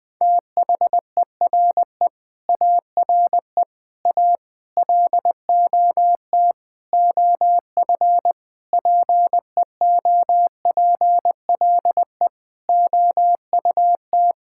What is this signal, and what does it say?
Morse code, heard as a keyed tone, about 20 wpm, 700 Hz. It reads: THERE ARE A LOT OF PEOPLE OUT
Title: Sentences with 8-Word Phrases - 20wpm